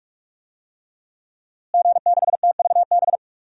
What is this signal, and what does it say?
Morse code, heard as a keyed tone, about 46 wpm, 700 Hz. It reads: G6TVB
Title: DXCC-100 Call Signs - 5 Characters - Repeated 3x - 45wpm